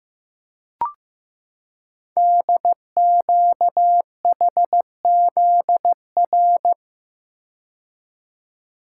Morse code, heard as a keyed tone, about 15 wpm, 700 Hz. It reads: DQHZR
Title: Code Groups - 5 Letter:Numbers - 15wpm